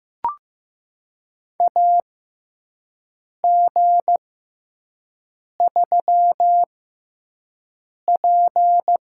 Morse code, heard as a keyed tone, about 15 wpm, spaced as for 5 wpm, 700 Hz. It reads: AG3P
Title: USA Call Signs - 4 Characters - 6x Character Spacing - 15wpm